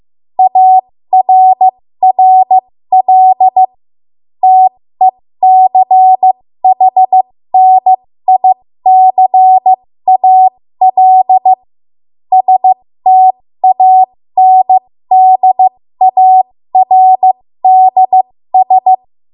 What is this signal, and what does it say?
Morse code, heard as a keyed tone, about 15 wpm, spaced as for 13 wpm, 750 Hz. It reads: ARRL TECHNICAL STANDARDS
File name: Morse / 13WPM